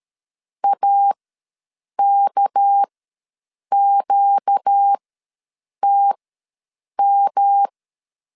Morse code, heard as a keyed tone, about 13 wpm, spaced as for 7 wpm, 800 Hz. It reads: AKQTM
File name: LEZIONI / 7 GF9Z / 7 G ws 12wpm 3140